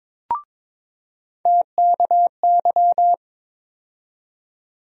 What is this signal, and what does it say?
Morse code, heard as a keyed tone, about 22 wpm, 700 Hz. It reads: TKY
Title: USA Call Sign Postfixes - 3 Letters 22wpm